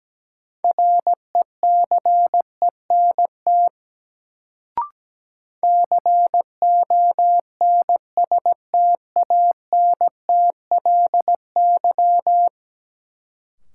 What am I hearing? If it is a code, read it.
Morse code, heard as a keyed tone, about 17 wpm, 700 Hz. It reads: RECENT CONSTANTLY
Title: Sentences from Top 1000 Words - Review Part 7 - New Word Drill - 17wpm